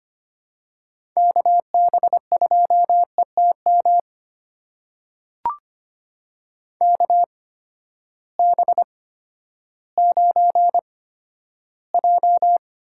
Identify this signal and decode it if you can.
Morse code, heard as a keyed tone, about 25 wpm, spaced as for 12 wpm, 700 Hz. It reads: KB2ETM KB9J
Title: USA Call Signs - 6 Characters - 8x Character Spacing - 25wpm